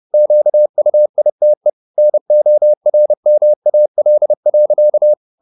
Morse code, heard as a keyed tone, about 30 wpm, 600 Hz. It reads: QUITE NORMAL.
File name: the old man and the sea / the old man and the sea 1 30wpm